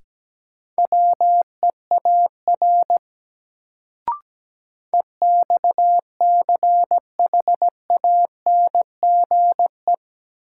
Morse code, heard as a keyed tone, about 17 wpm, 700 Hz. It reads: WEAR EXCHANGE
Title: Sentences from Top 1000 Words - Review Part 6 - New Word Drill - 17wpm